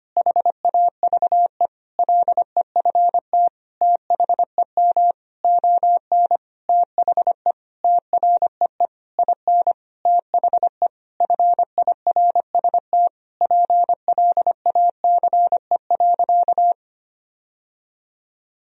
Morse code, heard as a keyed tone, about 25 wpm, 700 Hz. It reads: HAVE LEFT THEM ON THE TREE IN THE FIRST PLACE.
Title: Sentences from Top 1000 Words - Review Part 1 - 25wpm